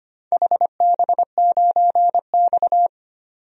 Morse code, heard as a keyed tone, about 25 wpm, 700 Hz. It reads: HB9X
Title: DXCC-100 Call Signs - 4 Characters - 25wpm